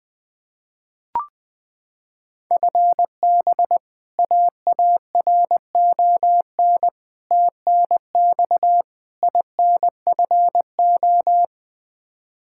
Morse code, heard as a keyed tone, about 21 wpm, 700 Hz. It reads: FB AARON TNX INFO
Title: Practice QSO - Exchange 3 - More Info - 20wpm